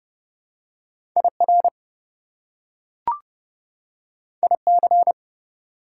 Morse code, heard as a keyed tone, about 29 wpm, 700 Hz. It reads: IR IC
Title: Prefixes and Suffixes 30wpm